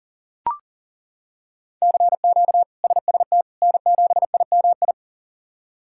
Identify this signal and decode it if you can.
Morse code, heard as a keyed tone, about 40 wpm, 690 Hz. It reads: CQ SST N7IMI
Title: K1USN SST QSO Practice - 40wpm